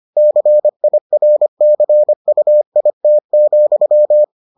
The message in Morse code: CIRCUIT,

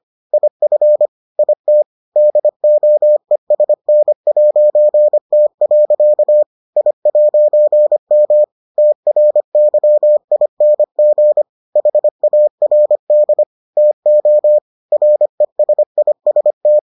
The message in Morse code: IF IT DOESN'T. I'M TRYING HARD TO RESIST

